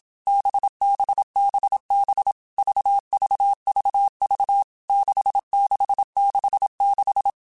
Morse code, heard as a keyed tone, about 27 wpm, 800 Hz. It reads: BBBB VVVV 6666
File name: Basic Letter Groups By7Groups / VB46 / VB46@25